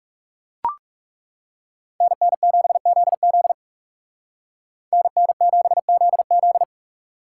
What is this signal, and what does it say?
Morse code, heard as a keyed tone, about 38 wpm, 700 Hz. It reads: NN7ZZ NN7ZZ